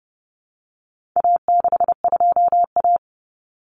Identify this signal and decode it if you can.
Morse code, heard as a keyed tone, about 30 wpm, 700 Hz. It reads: A62A
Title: Common Cut Numbers - 4-Digit 30wpm